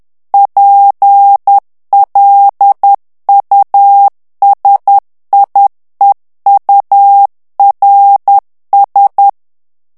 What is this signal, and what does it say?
Morse code, heard as a keyed tone, about 10 wpm, 800 Hz. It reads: PLUSIEURS
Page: Français - texte en clair 1 - vitesse : 10 mots minute